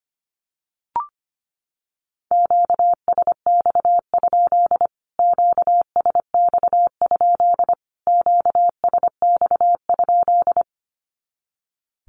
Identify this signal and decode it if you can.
Morse code, heard as a keyed tone, about 25 wpm, 700 Hz. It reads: QSX? QSX? QSX?
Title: Q-Codes Repeated 3x 25wpm